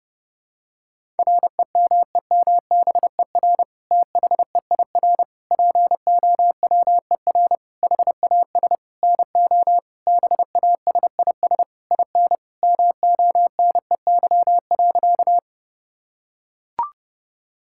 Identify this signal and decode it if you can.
Morse code, heard as a keyed tone, about 31 wpm, 700 Hz. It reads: REMEMBER THEIR POWER HAS NO BASIS IN MONEY.
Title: Sentences from Top 1000 Words - Review Part 5 - 30wpm